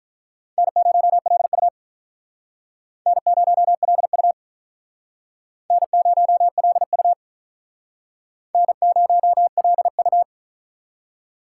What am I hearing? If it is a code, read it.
Morse code, heard as a keyed tone, about 43 wpm, 700 Hz. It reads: N0LU N0LU N0LU N0LU
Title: Speed Racing - USA 1x2 Call Signs 35wpm